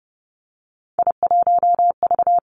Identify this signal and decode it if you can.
Morse code, heard as a keyed tone, about 30 wpm, 700 Hz. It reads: I1V
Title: Three Challenging Letter-Number Combinations 30wpm